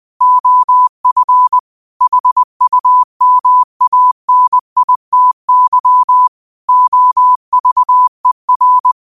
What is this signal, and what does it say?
Morse code, heard as a keyed tone, about 20 wpm, 1000 Hz. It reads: OF HUMANITY OVER